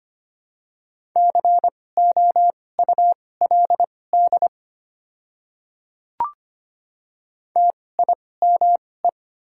Morse code, heard as a keyed tone, about 25 wpm, spaced as for 18 wpm, 700 Hz. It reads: COULD TIME
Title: 2x Character Spacing - Top 100 Words - 25wpm